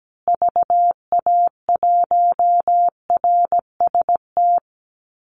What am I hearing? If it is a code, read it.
Morse code, heard as a keyed tone, about 17 wpm, 700 Hz. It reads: VA1RST
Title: Call Signs Repeated 2x - 17wpm